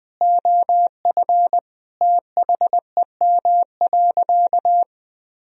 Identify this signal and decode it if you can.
Morse code, heard as a keyed tone, about 20 wpm, 700 Hz. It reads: OF THEM.
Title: Sentences from Top 1000 Words - Review Part 4 - 20wpm